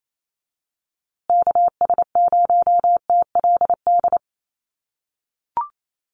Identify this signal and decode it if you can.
Morse code, heard as a keyed tone, about 28 wpm, 700 Hz. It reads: KS0TLD
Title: USA Call Signs Repeated 3x 28wpm